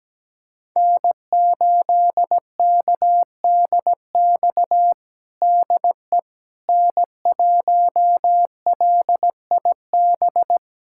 Morse code, heard as a keyed tone, about 17 wpm, 700 Hz. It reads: N8KDX DE N1LIB
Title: USA Callsign DE Callsign - 5 Characters - 17wpm